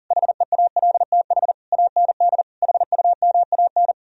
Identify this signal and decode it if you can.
Morse code, heard as a keyed tone, about 52 wpm, 700 Hz. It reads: HEALTH AND HUMAN